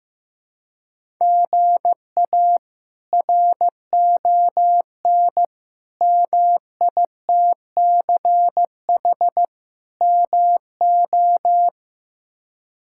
Morse code, heard as a keyed tone, about 15 wpm, 700 Hz. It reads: GA RON MITCH MO
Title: K1USN SST QSO Practice - 15wpm